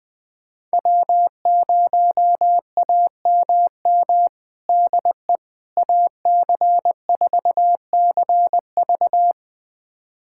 Morse code, heard as a keyed tone, about 20 wpm, 700 Hz. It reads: W0AMM DE AC4CV
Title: USA Callsign DE Callsign 20wpm